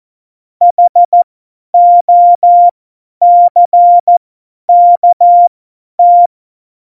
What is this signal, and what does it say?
Morse code, heard as a keyed tone, about 14 wpm, spaced as for 10 wpm, 700 Hz. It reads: HOCKT
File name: Morsekurs / Witze / Witz 03. November 2020, 50 F70